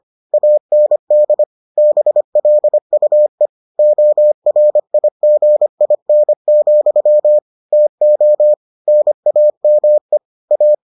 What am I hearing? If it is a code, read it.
Morse code, heard as a keyed tone, about 25 wpm, 600 Hz. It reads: AND BLUE ORIGIN, TO NAME A